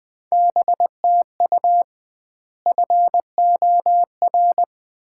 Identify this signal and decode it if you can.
Morse code, heard as a keyed tone, about 21 wpm, 700 Hz. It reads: BTU FOR
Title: Common QSO Phrases - 2x Word-Spacing - 20wpm